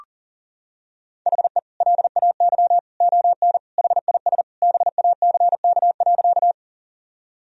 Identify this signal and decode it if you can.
Morse code, heard as a keyed tone, about 40 wpm, 700 Hz. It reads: HE LAY ON HIS BACK.
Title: Sentences from Top 1000 Words - Review Part 2 - 40wpm